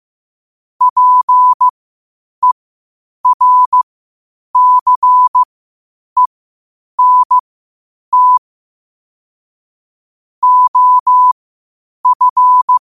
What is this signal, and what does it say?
Morse code, heard as a keyed tone, about 15 wpm, spaced as for 8 wpm, 1000 Hz. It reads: PERCENT OF
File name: qotdfiles / 8wpm